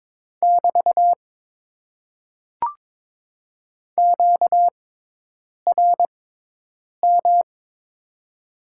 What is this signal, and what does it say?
Morse code, heard as a keyed tone, about 22 wpm, spaced as for 8 wpm, 700 Hz. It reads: <BT> QRM
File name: Top 90 QSO Elements - 6x Character Spacing - 22wpm